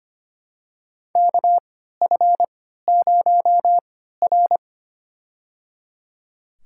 Morse code, heard as a keyed tone, about 25 wpm, spaced as for 14 wpm, 700 Hz. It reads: KF0R
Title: USA Call Signs - 4 Characters - 3x Character Spacing - 25wpm